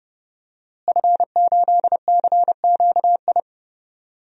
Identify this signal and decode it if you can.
Morse code, heard as a keyed tone, about 30 wpm, 700 Hz. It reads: F8CQI
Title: DXCC-100 Call Signs - 5 Characters - Repeated 3x - 30wpm